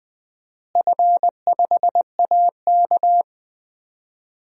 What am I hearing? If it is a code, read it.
Morse code, heard as a keyed tone, about 20 wpm, 700 Hz. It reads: F5AK
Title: DXCC-100 Call Signs - 4 Characters - Repeated 2x - 20wpm